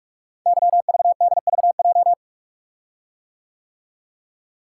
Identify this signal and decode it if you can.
Morse code, heard as a keyed tone, about 44 wpm, 690 Hz. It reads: YVDVJ
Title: Code Groups - 5 Letters - 45wpm